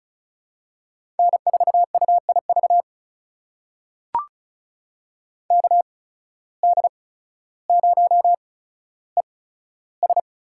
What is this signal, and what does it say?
Morse code, heard as a keyed tone, about 36 wpm, spaced as for 15 wpm, 700 Hz. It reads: N4UIV KD0ES
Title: USA Call Signs - 5 Characters - 8x Character Spacing - 35wpm